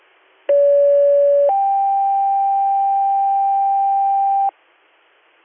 Two-tone page, heard as a one-second tone, then a three-second tone, about 570 then 790 Hz, with noise. It is heard from 0.5 to 4.5 s.